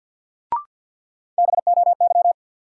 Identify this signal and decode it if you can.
Morse code, heard as a keyed tone, about 49 wpm, 700 Hz. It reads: BOY
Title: Sentences from Top 200 Words - New Word Drill - 50wpm